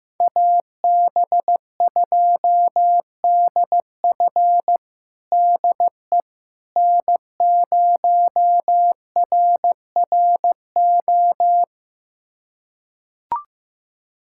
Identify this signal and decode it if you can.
Morse code, heard as a keyed tone, about 15 wpm, 700 Hz. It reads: AB2DF DE N0RRO
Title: USA Callsign DE Callsign 15wpm